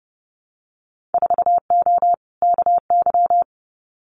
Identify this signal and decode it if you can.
Morse code, heard as a keyed tone, about 31 wpm, 700 Hz. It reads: 4O KY
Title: Winter Field Day Exchanges Encore 30wpm